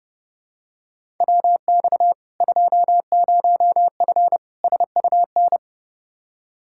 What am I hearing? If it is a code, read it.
Morse code, heard as a keyed tone, about 32 wpm, 700 Hz. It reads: WX 20F SUN
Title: Practice QSO - Exchange 2 - Info - 30wpm